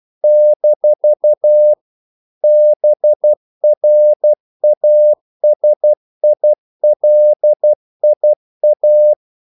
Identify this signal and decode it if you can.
Morse code, heard as a keyed tone, about 12 wpm, 600 Hz. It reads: - BRASILIA
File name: CW / Practice / NationalCapitals12